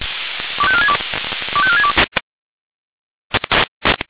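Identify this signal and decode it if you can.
Five-tone selective call (ZVEI), heard as quick runs of five tones, tones of about 70 ms. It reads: 25462 25462
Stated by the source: Five-tone ZVEI selective call received by SDR.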